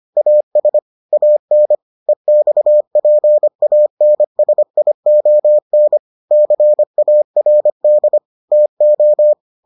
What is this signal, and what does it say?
Morse code, heard as a keyed tone, about 25 wpm, 600 Hz. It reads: AS AN EXPANSION CARD TO